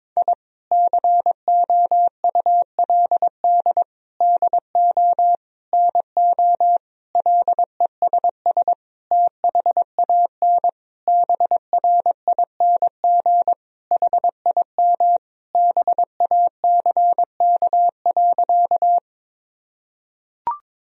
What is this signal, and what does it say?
Morse code, heard as a keyed tone, about 22 wpm, 700 Hz. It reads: I COULD DO NO LESS THAN BRING HIM BACK.
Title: Sentences from Top 1000 Words - Review Part 3 - 22wpm